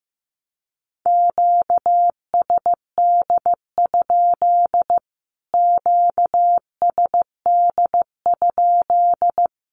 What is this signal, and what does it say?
Morse code, heard as a keyed tone, about 15 wpm, 700 Hz. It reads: QSD? QSD?
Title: Q-Codes Repeated 3x 15wpm